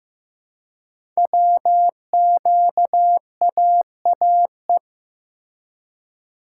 Morse code, heard as a keyed tone, about 15 wpm, 700 Hz. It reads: WQAAE
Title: Code Groups - 5 Letter:Numbers - 15wpm